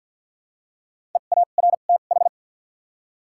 Morse code, heard as a keyed tone, about 53 wpm, spaced as for 37 wpm, 700 Hz. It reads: EARTH
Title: 2x Character Spacing - Top 300 Words - 50wpm